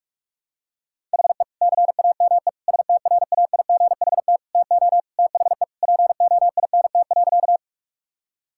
Morse code, heard as a keyed tone, about 41 wpm, 700 Hz. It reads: HE CAME STRAIGHT TO THE POINT.
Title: Sentences from Top 1000 Words - Review Part 6 - 45wpm